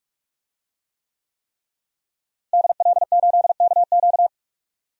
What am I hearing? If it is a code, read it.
Morse code, heard as a keyed tone, about 46 wpm, 700 Hz. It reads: DL8KQ
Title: DXCC-100 Call Signs - 5 Characters - Repeated 3x - 45wpm